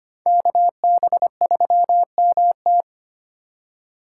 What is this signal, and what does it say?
Morse code, heard as a keyed tone, about 25 wpm, 700 Hz. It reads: KB3MT
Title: USA Call Signs - 4x Character Spacing - 25wpm